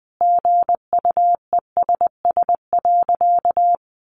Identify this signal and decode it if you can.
Morse code, heard as a keyed tone, about 19 wpm, 700 Hz. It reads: GUESS.
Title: Sentences from Top 1500 Words 20wpm